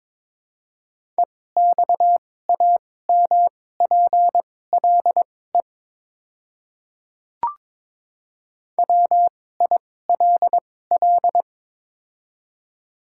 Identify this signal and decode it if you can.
Morse code, heard as a keyed tone, about 22 wpm, spaced as for 16 wpm, 700 Hz. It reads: EXAMPLE WILL
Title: 2x Character Spacing - Top 300 Words - 22wpm